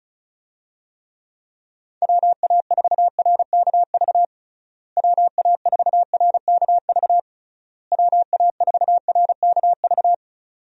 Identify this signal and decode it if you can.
Morse code, heard as a keyed tone, about 36 wpm, 700 Hz. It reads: WA4RKV WA4RKV WA4RKV
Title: USA Call Signs - 6 Characters - Repeated 3x - 35wpm